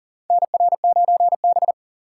Morse code, heard as a keyed tone, about 40 wpm, 700 Hz. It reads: NR9B
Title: USA Call Signs - 4 Characters - 3x Character Spacing - 40wpm